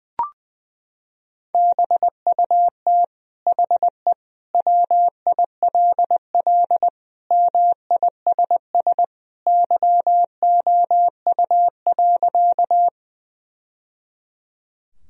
Morse code, heard as a keyed tone, about 21 wpm, 700 Hz. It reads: BUT HE WILL MISS YOU.
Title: Sentences from Top 1000 Words - Review Part 3 - 20wpm